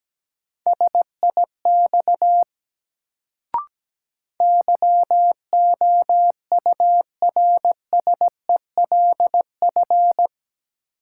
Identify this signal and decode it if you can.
Morse code, heard as a keyed tone, about 17 wpm, 700 Hz. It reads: SIX YOURSELF
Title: Sentences from Top 600 Words - New Word Drill - 17wpm